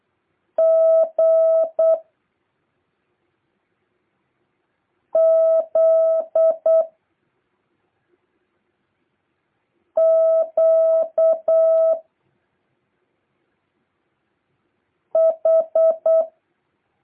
Morse code, heard as a keyed tone, about 8 wpm, 650 Hz. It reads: G Z Q H
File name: jh8cbh / 2023cw